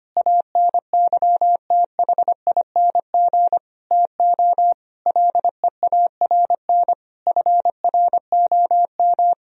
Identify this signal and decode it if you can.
Morse code, heard as a keyed tone, about 25 wpm, 700 Hz. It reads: ANYTHING TO LEARN FROM
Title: Sentences from Top 1000 Words - Review Part 1 - 25wpm